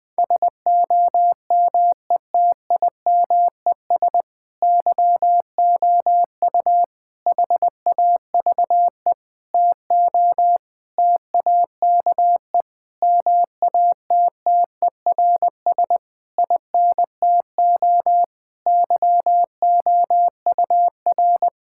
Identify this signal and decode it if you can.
Morse code, heard as a keyed tone, about 20 wpm, 700 Hz. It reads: SOMETIMES YOU HAVE TO TAKE MATTERS INTO YOUR
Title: Sentences with 7-Word Phrases - 20wpm